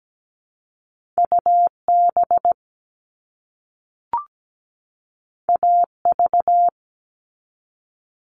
Morse code, heard as a keyed tone, about 17 wpm, 700 Hz. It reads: UB AV
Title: All Cut Numbers - 2-Digit 17wpm